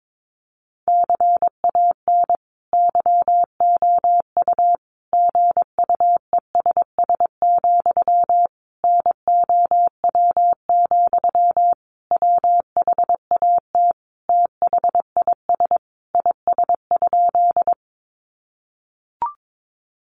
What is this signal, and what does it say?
Morse code, heard as a keyed tone, about 22 wpm, 700 Hz. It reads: CAN YOU GUESS, NOW, WHAT THIS IS?
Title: Sentences from Top 1500 Words 22wpm